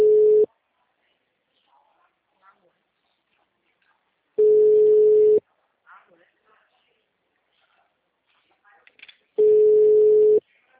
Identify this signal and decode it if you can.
Ringback tone (Europe, 425 Hz), with speech.
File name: out- -2019- - - .44446